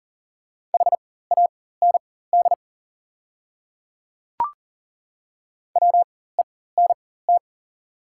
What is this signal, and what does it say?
Morse code, heard as a keyed tone, about 38 wpm, spaced as for 18 wpm, 700 Hz. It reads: HAND WENT
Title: 4x Character Spacing - Top 300 Words - 40wpm